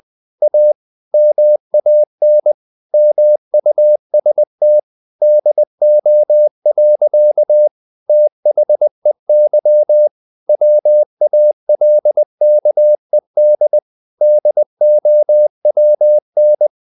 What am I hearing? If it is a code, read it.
Morse code, heard as a keyed tone, about 20 wpm, 600 Hz. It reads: A MAN MUST DO. THEY WALKED DOWN
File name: the old man and the sea / the old man and the sea part 1 20wpm